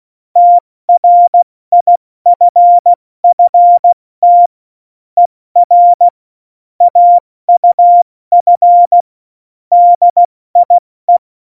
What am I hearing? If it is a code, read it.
Morse code, heard as a keyed tone, about 16 wpm, 700 Hz. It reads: TRIFFT ER AUF DIE
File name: Morsekurs / Witze / Witz 12. Dezember 2017, 70 F80